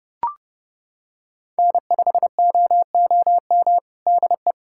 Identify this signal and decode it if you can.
Morse code, heard as a keyed tone, about 32 wpm, 700 Hz. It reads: N5OOM DE
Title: USA Callsign DE Callsign 30wpm